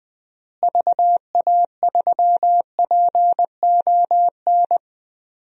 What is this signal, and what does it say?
Morse code, heard as a keyed tone, about 20 wpm, 700 Hz. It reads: VA3PON
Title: DXCC-100 Call Signs - 6 Characters - Repeated 3x - 20wpm